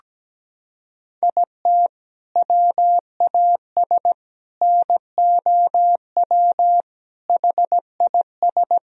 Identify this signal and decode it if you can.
Morse code, heard as a keyed tone, about 17 wpm, 700 Hz. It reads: IT WAS NOW HIS
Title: Sentences from Top 200 Words and Abbreviations 17wpm